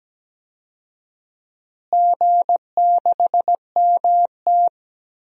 Morse code, heard as a keyed tone, about 17 wpm, 700 Hz. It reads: G6MT
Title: DXCC-100 Call Signs - 4 Characters - Repeated 3x - 17wpm